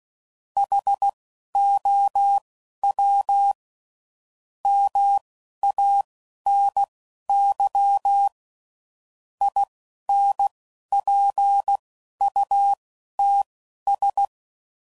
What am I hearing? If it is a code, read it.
Morse code, heard as a keyed tone, about 16 wpm, spaced as for 11 wpm, 800 Hz. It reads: HOW MANY INPUTS
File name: CW Code Groups @10WPM / Technical Subjects / OR Gate